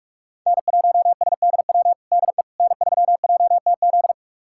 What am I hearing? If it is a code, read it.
Morse code, heard as a keyed tone, about 44 wpm, 700 Hz. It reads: N1SDW DE N3JTZ
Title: USA Callsign DE Callsign - 5 Characters - 45wpm